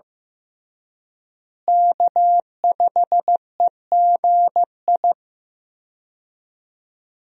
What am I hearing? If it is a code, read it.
Morse code, heard as a keyed tone, about 15 wpm, 700 Hz. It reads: K5EGI
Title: USA Call Signs - 5 Characters - Repeated 2x - 15wpm